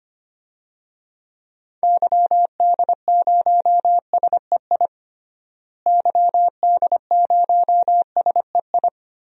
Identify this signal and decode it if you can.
Morse code, heard as a keyed tone, about 25 wpm, 700 Hz. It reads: YD0SEI YD0SEI
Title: DXCC-100 Call Signs - 6 Characters - Repeated 2x - 25wpm